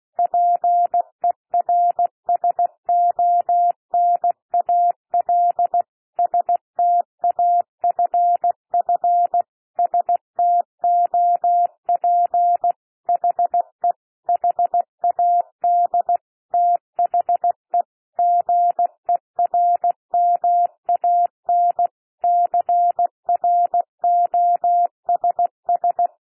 Morse code, heard as a keyed tone, about 16 wpm, 700 Hz. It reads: PERSONAL STAFF STOP HE HAD THE GERMAN CROSS
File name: cwmp3 / P1 SOUND PASSAGE 16 wpm